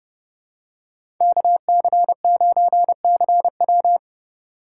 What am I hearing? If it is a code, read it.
Morse code, heard as a keyed tone, about 32 wpm, 700 Hz. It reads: KC9CW
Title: USA Call Signs - 5 Characters - 2x Character Spacing - 30wpm